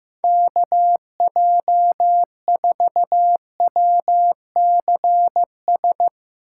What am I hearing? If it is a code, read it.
Morse code, heard as a keyed tone, about 15 wpm, 700 Hz. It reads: KJ4WCS
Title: Speed Racing - USA 2x3 Call Signs 15wpm